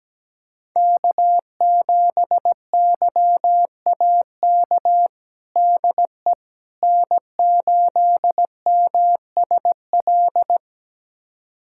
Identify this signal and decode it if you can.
Morse code, heard as a keyed tone, about 17 wpm, 700 Hz. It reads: K7YAK DE N8MSL
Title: USA Callsign DE Callsign - 5 Characters - 17wpm